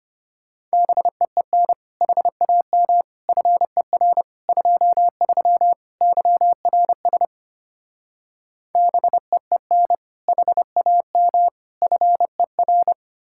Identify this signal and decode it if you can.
Morse code, heard as a keyed tone, about 32 wpm, 700 Hz. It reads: BEEN HAM FER 23 YRS BEEN HAM FER